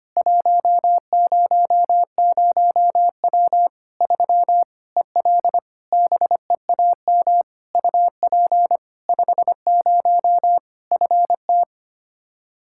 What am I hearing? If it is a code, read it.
Morse code, heard as a keyed tone, about 25 wpm, 700 Hz. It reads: 100W 3 EL BEAM UP 50 FT
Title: K1USN SST QSO Practice 25wpm